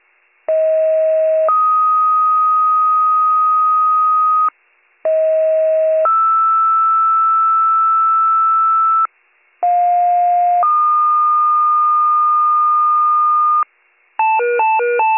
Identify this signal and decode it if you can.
Two-tone page, heard as a one-second tone, then a three-second tone, about 640 then 1250 Hz, about 640 then 1320 Hz, about 710 then 1220 Hz. It is heard from 0.5 to 13.6 s.